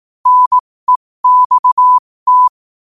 Morse code, heard as a keyed tone, about 18 wpm, spaced as for 15 wpm, 1000 Hz. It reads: NEXT
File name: 15wpm